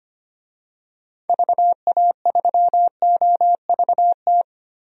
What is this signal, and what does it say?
Morse code, heard as a keyed tone, about 25 wpm, 700 Hz. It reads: VA3OVT
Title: DXCC-100 Call Signs - 6 Characters - Repeated 3x - 25wpm